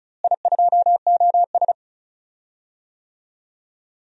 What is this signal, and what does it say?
Morse code, heard as a keyed tone, about 36 wpm, 700 Hz. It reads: I2OS
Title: DXCC-100 Call Signs - 4 Characters - Repeated 3x - 35wpm